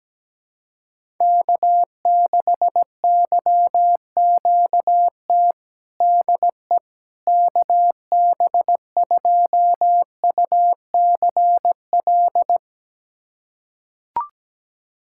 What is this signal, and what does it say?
Morse code, heard as a keyed tone, about 17 wpm, 700 Hz. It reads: K6YQT DE KB2UCL
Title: USA Callsign DE Callsign 17wpm